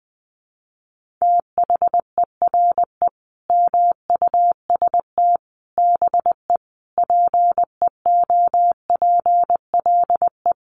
Morse code, heard as a keyed tone, about 20 wpm, 700 Hz. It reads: THERE MUST BE PEOPLE
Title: Sentences from Top 1400 Words 20wpm